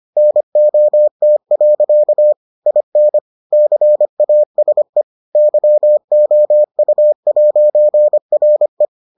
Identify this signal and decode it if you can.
Morse code, heard as a keyed tone, about 25 wpm, 600 Hz. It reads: NOT. IN CASE YOU'RE